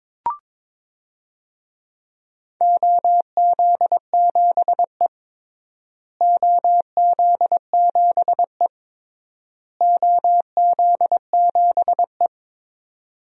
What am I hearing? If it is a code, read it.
Morse code, heard as a keyed tone, about 22 wpm, 700 Hz. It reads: OZ7E OZ7E OZ7E
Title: DXCC-100 Call Signs - 4 Characters - Repeated 3x - 22wpm